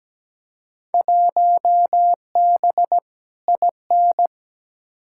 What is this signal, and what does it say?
Morse code, heard as a keyed tone, about 17 wpm, 700 Hz. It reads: 1B IN
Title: Field Day Exchanges Encore - 17wpm